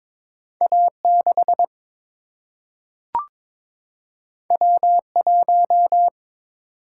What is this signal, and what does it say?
Morse code, heard as a keyed tone, about 22 wpm, 700 Hz. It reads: A6 W1
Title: Single-Letter Single-Number 22wpm